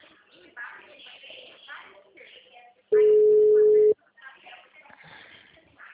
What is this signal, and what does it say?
Ringback tone (Europe, 425 Hz), with speech.